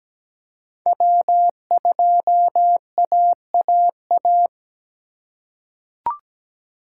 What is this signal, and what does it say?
Morse code, heard as a keyed tone, about 17 wpm, 700 Hz. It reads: W2AAA
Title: USA Call Signs - 4x Character Spacing - 17wpm